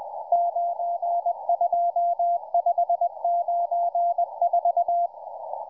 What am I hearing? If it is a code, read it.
Morse code, heard as a keyed tone, about 20 wpm, 690 Hz, with radio noise. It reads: 92594